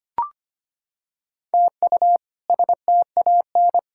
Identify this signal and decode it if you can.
Morse code, heard as a keyed tone, about 25 wpm, 700 Hz. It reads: TU STAN
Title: K1USN SST QSO Practice - 25wpm